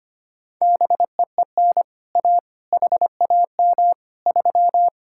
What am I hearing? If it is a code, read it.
Morse code, heard as a keyed tone, about 25 wpm, 700 Hz. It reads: BEEN A HAM 3
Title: Speed-Racing Common QSO Phrases Encore - 25wpm